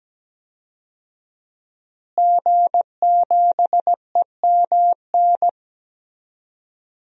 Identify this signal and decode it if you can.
Morse code, heard as a keyed tone, about 17 wpm, 700 Hz. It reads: G7EMN
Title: DXCC-100 Call Signs - 5 Characters - Repeated 3x - 17wpm